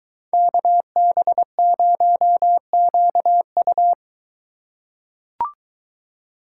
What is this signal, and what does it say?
Morse code, heard as a keyed tone, about 22 wpm, 700 Hz. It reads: KB0QU